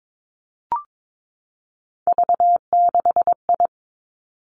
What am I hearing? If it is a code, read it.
Morse code, heard as a keyed tone, about 22 wpm, 700 Hz. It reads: V6I
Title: Three Challenging Letter-Number Combinations 22wpm